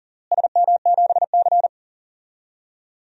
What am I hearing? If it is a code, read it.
Morse code, heard as a keyed tone, about 40 wpm, 700 Hz. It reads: SM7C
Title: DXCC-100 Call Signs - 4 Characters - Repeated 3x - 40wpm